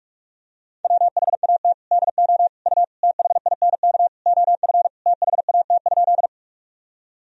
Morse code, heard as a keyed tone, about 42 wpm, 700 Hz. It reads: WHAT DO U THINK OF THAT?
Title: Sentences from Top 200 Words and Abbreviations 45wpm